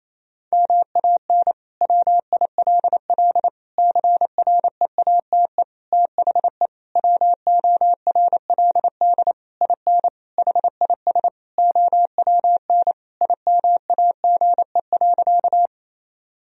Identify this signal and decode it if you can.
Morse code, heard as a keyed tone, about 28 wpm, 700 Hz. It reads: MAN WILL CREATE THE WORLD IN HIS OWN IMAGE.
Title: Sentences from Top 1000 Words - Review Part 4 - 28wpm